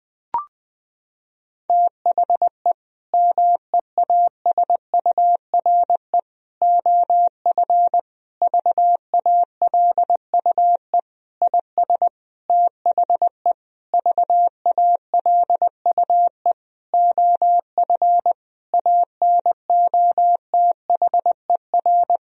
Morse code, heard as a keyed tone, about 20 wpm, 700 Hz. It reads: THE MEASURE OF VALUE IS THE VALUE OF ANOTHER
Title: Sentences from Top 1000 Words - Review Part 3 - 20wpm